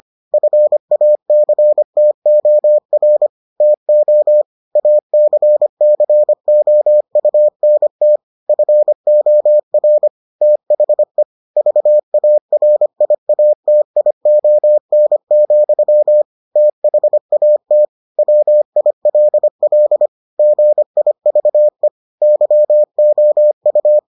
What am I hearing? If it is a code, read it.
Morse code, heard as a keyed tone, about 25 wpm, 600 Hz. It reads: FACTOR TO ACCOUNT FOR THE VARIATION, THAT WILL GIVE YOU